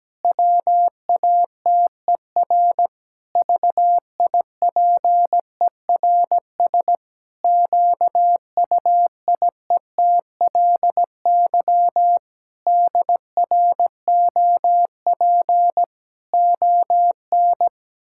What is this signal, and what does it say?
Morse code, heard as a keyed tone, about 17 wpm, 700 Hz. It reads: WATER VIPERS QUIETLY DROP ON
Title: Pangram Challenge v2 - 17wpm